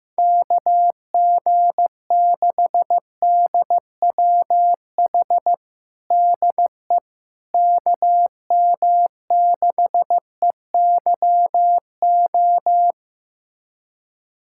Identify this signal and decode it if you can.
Morse code, heard as a keyed tone, about 15 wpm, 700 Hz. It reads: KG6DWH DE KM6EYO
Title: USA Callsign DE Callsign - 6 Characters - 15wpm